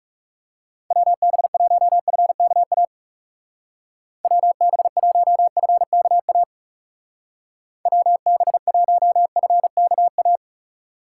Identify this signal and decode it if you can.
Morse code, heard as a keyed tone, about 37 wpm, 700 Hz. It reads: WB1FKA WB1FKA WB1FKA
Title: Speed Racing - USA Call Signs 30wpm